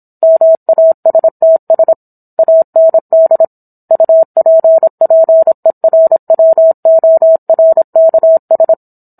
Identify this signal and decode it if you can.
Morse code, heard as a keyed tone, about 27 wpm, 650 Hz. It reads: MASTS AND UPPERWORKS